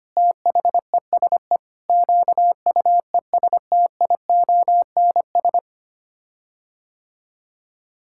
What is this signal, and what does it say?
Morse code, heard as a keyed tone, about 25 wpm, 700 Hz. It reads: THESE QUESTIONS
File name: Sets of 9 Words - 25wpm